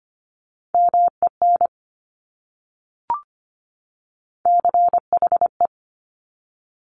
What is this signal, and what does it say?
Morse code, heard as a keyed tone, about 25 wpm, 700 Hz. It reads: MEN CHE
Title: Top Three Letter Combinations in Words 25wpm